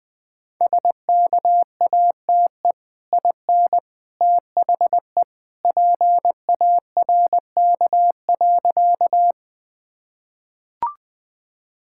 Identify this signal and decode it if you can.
Morse code, heard as a keyed tone, about 20 wpm, 700 Hz. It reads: SKATE IN THE PARK.
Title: Easy Sentences with Hints - 20wpm